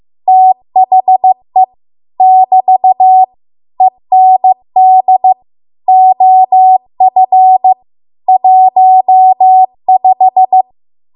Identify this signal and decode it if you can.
Morse code, heard as a keyed tone, about 15 wpm, 750 Hz. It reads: THE <BT> END OF 15